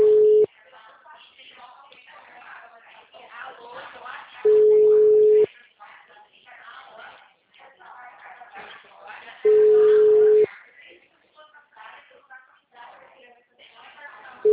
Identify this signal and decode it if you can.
Ringback tone (Europe, 425 Hz), with speech.